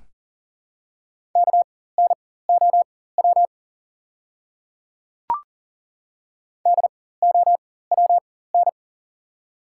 Morse code, heard as a keyed tone, about 40 wpm, spaced as for 19 wpm, 690 Hz. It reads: KNOW DOWN
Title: 4x Character Spacing - Top 200 Words - 40wpm